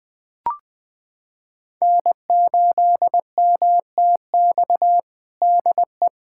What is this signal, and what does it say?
Morse code, heard as a keyed tone, about 21 wpm, 700 Hz. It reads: N8MTX DE
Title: USA Callsign DE Callsign - 5 Characters - 20wpm